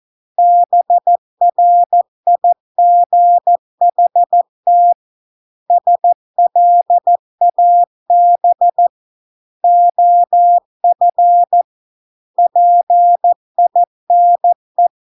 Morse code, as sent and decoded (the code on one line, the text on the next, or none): -... .-. .. --. .... - / ... .-.. .- -... / --- ..-. / .--. .. -. .
BRIGHT SLAB OF PINE